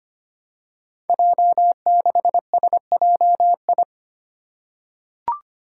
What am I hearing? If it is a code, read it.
Morse code, heard as a keyed tone, about 25 wpm, 700 Hz. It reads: J6SJI
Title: Five Challenging Letter-Number Combinations 25wpm